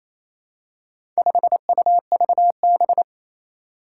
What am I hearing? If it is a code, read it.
Morse code, heard as a keyed tone, about 27 wpm, 700 Hz. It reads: 5UVB
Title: Four Challenging Letter-Number Combinations 28wpm